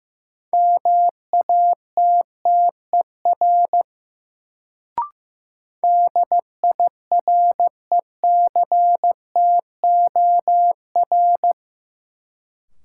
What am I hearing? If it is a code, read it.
Morse code, heard as a keyed tone, about 15 wpm, 700 Hz. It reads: MATTER DIRECTOR
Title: Sets of 3 Words - New Word Drill - 15wpm